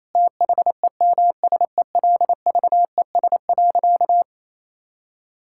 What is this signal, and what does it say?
Morse code, heard as a keyed tone, about 28 wpm, 700 Hz. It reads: THEMSELVES.
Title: Sentences from Top 1000 Words - Review Part 5 - 28wpm